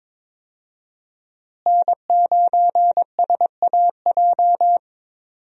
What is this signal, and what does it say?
Morse code, heard as a keyed tone, about 22 wpm, 700 Hz. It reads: N9SAJ
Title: USA Call Signs - 5 Characters - Repeated 3x - 22wpm